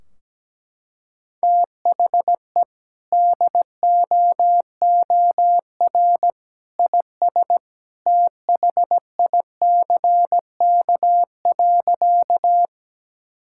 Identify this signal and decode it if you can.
Morse code, heard as a keyed tone, about 17 wpm, 700 Hz. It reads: THE DOOR IS THICK.
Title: Sentences from Top 1600 Words 17wpm